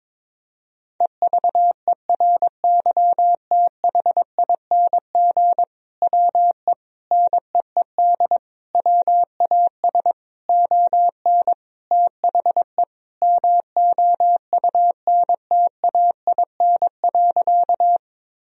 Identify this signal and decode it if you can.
Morse code, heard as a keyed tone, about 22 wpm, 700 Hz. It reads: EVERYTHING WE NEED WAS ON THE MOUNTAIN.
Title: Sentences from Top 1000 Words - Review Part 1 - 22wpm